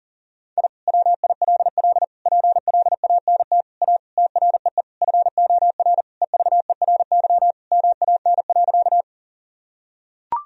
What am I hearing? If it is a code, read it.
Morse code, heard as a keyed tone, about 40 wpm, 700 Hz. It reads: I WILL PLANT A TREE FOR EVERY MAN.
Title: Sentences from Top 1000 Words - Review Part 1 - 40wpm